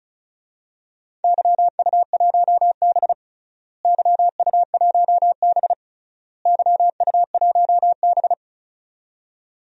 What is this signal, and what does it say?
Morse code, heard as a keyed tone, about 36 wpm, 700 Hz. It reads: YU1B YU1B YU1B
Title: DXCC-100 Call Signs - 4 Characters - Repeated 3x - 35wpm